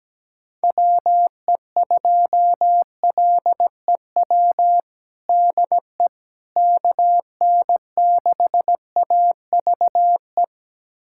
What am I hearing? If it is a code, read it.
Morse code, heard as a keyed tone, about 17 wpm, 700 Hz. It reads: WE2LEW DE KN6AVE
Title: USA Callsign DE Callsign - 6 Characters - 17wpm